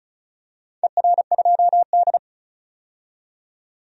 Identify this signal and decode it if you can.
Morse code, heard as a keyed tone, about 36 wpm, 700 Hz. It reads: ER2D
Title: DXCC-100 Call Signs - 4 Characters - 35wpm